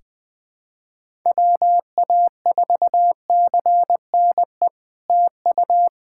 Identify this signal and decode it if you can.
Morse code, heard as a keyed tone, about 21 wpm, 700 Hz. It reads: WA4CNE TU
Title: SOTA QSO Practice 20wpm